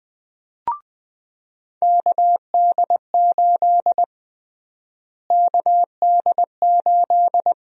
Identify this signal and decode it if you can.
Morse code, heard as a keyed tone, about 20 wpm, 700 Hz. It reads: KD8 KD8
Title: USA Call Sign Prefixes - Repeated 2x - 20wpm